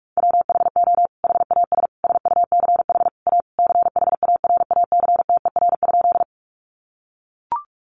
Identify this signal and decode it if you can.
Morse code, heard as a keyed tone, about 44 wpm, 700 Hz. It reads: WHO HAS SUCH A CHARACTER?
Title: Sentences from Top 1200 Words 45wpm